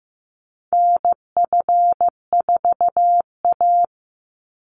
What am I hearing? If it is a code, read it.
Morse code, heard as a keyed tone, about 15 wpm, 700 Hz. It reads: NF4A
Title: Call Signs Repeated 3x - 15wpm